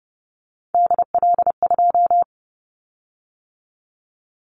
Morse code, heard as a keyed tone, about 30 wpm, 700 Hz. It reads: DL2
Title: Callsign Prefixes for Top 50 DXCC - 30wpm